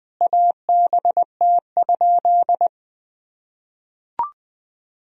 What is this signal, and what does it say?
Morse code, heard as a keyed tone, about 21 wpm, 700 Hz. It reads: ABT?
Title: Sentences from Top 100 Words and Abbreviations 20wpm